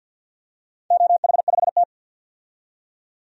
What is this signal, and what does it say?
Morse code, heard as a keyed tone, about 50 wpm, 700 Hz. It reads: OH5T
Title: DXCC-100 Call Signs - 4 Characters - 50wpm